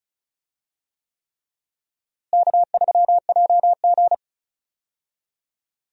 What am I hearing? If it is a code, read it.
Morse code, heard as a keyed tone, about 34 wpm, 700 Hz. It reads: K3JG
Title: USA Call Signs - 4 Characters - Repeated 3x - 35wpm